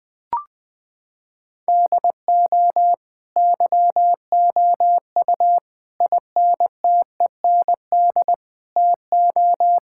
Morse code, heard as a keyed tone, about 20 wpm, 700 Hz. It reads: DO YOU INTEND TO
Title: Sentences from Top 1600 Words 20wpm